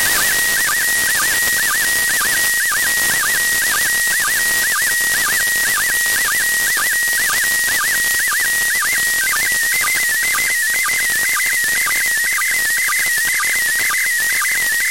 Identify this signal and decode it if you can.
SSTV (slow-scan television), PD 120.
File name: ISS-SSTV-audio 145802924Hz